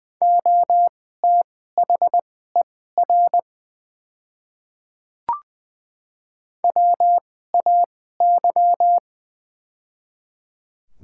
Morse code, heard as a keyed tone, about 20 wpm, spaced as for 14 wpm, 700 Hz. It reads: OTHER WAY
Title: 2x Character Spacing - Top 100 Words - 20wpm